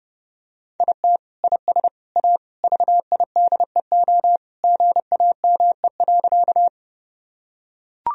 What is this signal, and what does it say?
Morse code, heard as a keyed tone, about 31 wpm, 700 Hz. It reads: IT IS A VIDEO GAME.
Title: Sentences from Top 1000 Words - Review Part 4 - 30wpm